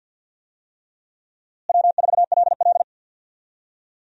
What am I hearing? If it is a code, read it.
Morse code, heard as a keyed tone, about 51 wpm, 700 Hz. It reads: W4LL
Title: USA Call Signs - 4 Characters - Repeated 3x - 50wpm